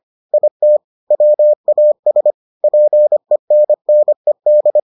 Morse code, heard as a keyed tone, about 25 wpm, 600 Hz. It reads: IT WAS PENNED